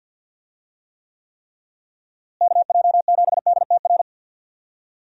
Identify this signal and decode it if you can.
Morse code, heard as a keyed tone, about 52 wpm, 700 Hz. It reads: KJ7DTR
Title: USA Call Signs - 6 Characters - Repeated 3x - 50wpm